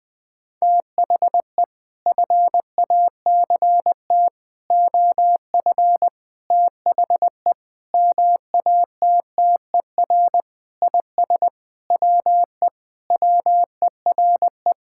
Morse code, heard as a keyed tone, about 19 wpm, 700 Hz. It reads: THE FACT OF THE MATTER IS WE WERE
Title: Sentences with 5-Word Phrases - 20wpm